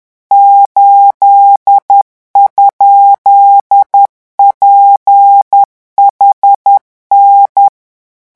Morse code, heard as a keyed tone, about 10 wpm, 800 Hz. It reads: 8?PHN
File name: morse mp3 / revision13-vitesse 10 mots